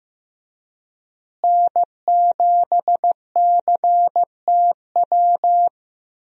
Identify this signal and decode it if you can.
Morse code, heard as a keyed tone, about 15 wpm, 700 Hz. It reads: N7CTW
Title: USA Call Signs - 5 Characters - Repeated 3x - 15wpm